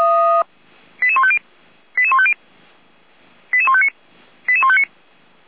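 Five-tone selective call (ZVEI), heard as quick runs of five tones, tones of about 70 ms. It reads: 88150 88150 88159 88159
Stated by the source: ZVEI five-tone call received by SDR.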